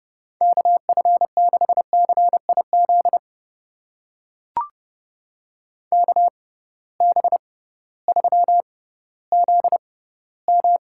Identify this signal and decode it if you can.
Morse code, heard as a keyed tone, about 32 wpm, spaced as for 16 wpm, 700 Hz. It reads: KF6CIZ KB3ZM
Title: USA Call Signs - 6 Characters - 6x Character Spacing - 30wpm